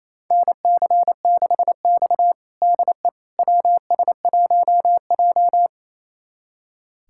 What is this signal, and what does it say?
Morse code, heard as a keyed tone, about 28 wpm, 700 Hz. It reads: NC6X DE WS1J
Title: USA Callsign DE Callsign - 4 Characters - 28wpm